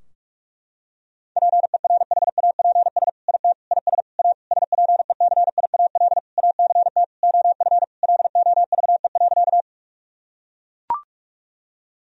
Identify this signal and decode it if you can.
Morse code, heard as a keyed tone, about 41 wpm, 700 Hz. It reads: PERHAPS IT IS A SPECIAL ACT OF LOVE.
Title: Sentences from Top 1000 Words - Review Part 3 - 45wpm